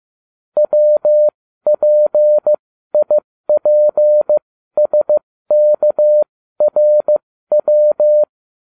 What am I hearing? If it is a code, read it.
Morse code, heard as a keyed tone, about 15 wpm, spaced as for 13 wpm, 600 Hz. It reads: WPIPSKRW